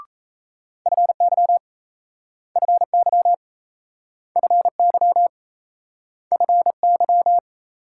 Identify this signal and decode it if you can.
Morse code, heard as a keyed tone, about 38 wpm, 700 Hz. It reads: FY FY FY FY